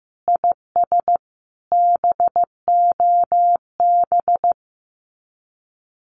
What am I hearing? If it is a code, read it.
Morse code, heard as a keyed tone, about 15 wpm, 700 Hz. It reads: IS BOB
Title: Common QSO Phrases 15wpm